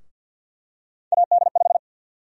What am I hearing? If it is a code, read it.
Morse code, heard as a keyed tone, about 53 wpm, 700 Hz. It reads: AD5
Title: USA Call Sign Prefixes - 50wpm